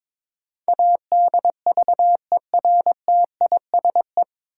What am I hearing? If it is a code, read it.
Morse code, heard as a keyed tone, about 22 wpm, 700 Hz. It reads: ADVERTISE
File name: 9 Letter Words - First 3 Letters Given - 22wpm